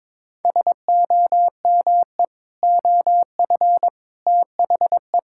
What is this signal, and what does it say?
Morse code, heard as a keyed tone, about 22 wpm, 700 Hz. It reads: SOME OF THE
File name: practice / 22wpm / Sentences with 3-Word Phrases - 22wpm